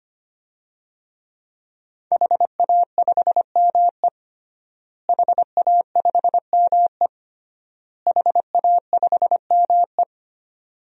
Morse code, heard as a keyed tone, about 25 wpm, 700 Hz. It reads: HA5ME HA5ME HA5ME
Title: DXCC-100 Call Signs - 5 Characters - Repeated 3x - 25wpm